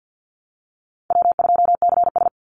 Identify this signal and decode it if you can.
Morse code, heard as a keyed tone, about 50 wpm, 700 Hz. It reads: W2CS
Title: Easy Call Signs - 50wpm